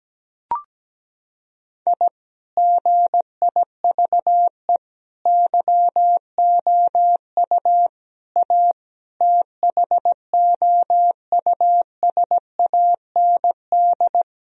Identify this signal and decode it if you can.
Morse code, heard as a keyed tone, about 17 wpm, 700 Hz. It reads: I GIVE YOU A THOUSAND
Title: Sentences from Top 1000 Words - Review Part 4 - 17wpm